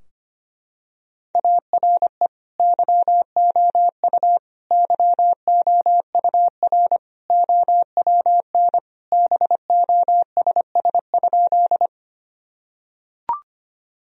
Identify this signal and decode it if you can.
Morse code, heard as a keyed tone, about 25 wpm, 700 Hz. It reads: ARE YOU YOUR OWN BOSS?
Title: Sentences from Top 1000 Words - Review Part 4 - 25wpm